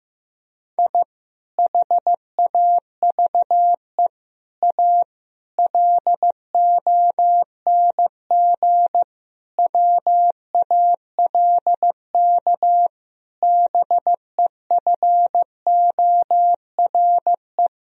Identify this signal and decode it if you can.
Morse code, heard as a keyed tone, about 15 wpm, 700 Hz. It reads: I HAVE A LONG WALK BEFORE
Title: Sentences from Top 1000 Words - Review Part 1 - 15wpm